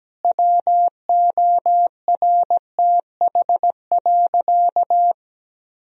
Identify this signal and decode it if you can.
Morse code, heard as a keyed tone, about 17 wpm, 700 Hz. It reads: WORTH.
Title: Sentences from Top 1000 Words - Review Part 4 - 17wpm